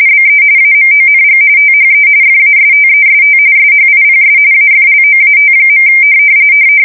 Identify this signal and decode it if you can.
RTTY, about 45.45 baud. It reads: BE AMONG THE FASTEST SHIPS AFLOAT.
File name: fsk170 / battleshipnj